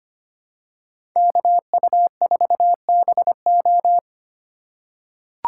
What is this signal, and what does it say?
Morse code, heard as a keyed tone, about 25 wpm, 700 Hz. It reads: KU4BO
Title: USA Call Signs - 5 Characters - 2x Character Spacing - 25wpm